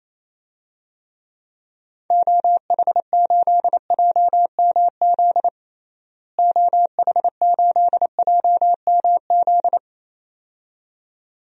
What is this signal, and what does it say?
Morse code, heard as a keyed tone, about 28 wpm, 700 Hz. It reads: OH8JMZ OH8JMZ
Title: DXCC-100 Call Signs - 6 Characters - Repeated 2x - 28wpm